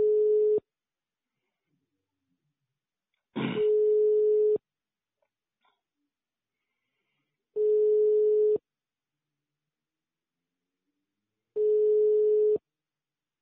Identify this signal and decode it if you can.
Ringback tone (425 Hz, 1 s on, 3 s off).
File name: sauvegarde / ACRRecordings- -jusquau 17-juin-2017 / UBO Banque De La Post pour ASBL 2017 05 22 13 30 03